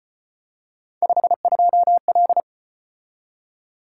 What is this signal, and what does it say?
Morse code, heard as a keyed tone, about 34 wpm, 700 Hz. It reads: 52L